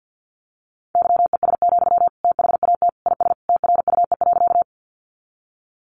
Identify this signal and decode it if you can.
Morse code, heard as a keyed tone, about 50 wpm, 700 Hz. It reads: YES, THAT IS TRUE.
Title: Sentences from Top 500 Words 50wpm